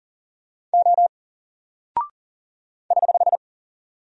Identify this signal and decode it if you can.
Morse code, heard as a keyed tone, about 40 wpm, 700 Hz. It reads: O <HH>
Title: Single Character (Letter:Number:Punctuation:Prosign) 40wpm